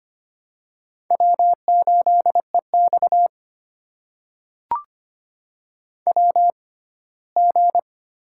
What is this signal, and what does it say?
Morse code, heard as a keyed tone, about 25 wpm, spaced as for 17 wpm, 700 Hz. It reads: W8EX WG
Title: USA Call Signs - 4 Characters - 6x Character Spacing - 25wpm